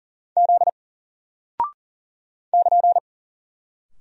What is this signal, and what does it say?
Morse code, heard as a keyed tone, about 39 wpm, 700 Hz. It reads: Z <KN>
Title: Single Character (Letter:Number:Punctuation:Prosign) 40wpm